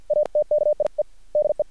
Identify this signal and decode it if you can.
Morse code, heard as a keyed tone, about 50 wpm, spaced as for 44 wpm, 600 Hz. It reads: ATKIE DE
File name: archived / CRSnet